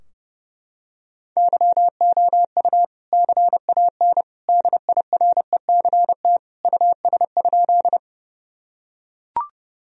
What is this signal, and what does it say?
Morse code, heard as a keyed tone, about 31 wpm, 700 Hz. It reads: YOU CAN DIRECT US?
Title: Sentences from Top 1000 Words - Review Part 2 - 30wpm